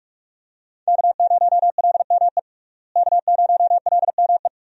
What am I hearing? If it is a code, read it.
Morse code, heard as a keyed tone, about 41 wpm, 700 Hz. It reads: K0LME K0LME
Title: USA Call Signs - 5 Characters - Repeated 2x - 45wpm